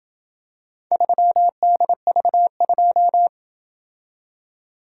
Morse code, heard as a keyed tone, about 26 wpm, 700 Hz. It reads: 3DV2